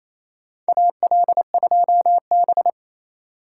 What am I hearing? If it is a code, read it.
Morse code, heard as a keyed tone, about 28 wpm, 700 Hz. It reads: AL2B
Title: USA Call Signs - 4 Characters - 3x Character Spacing - 28wpm